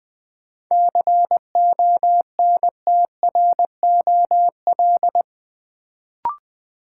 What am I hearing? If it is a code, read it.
Morse code, heard as a keyed tone, about 20 wpm, 700 Hz. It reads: CONTROL
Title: Sentences from Top 1000 Words - Review Part 2 - New Word Drill - 20wpm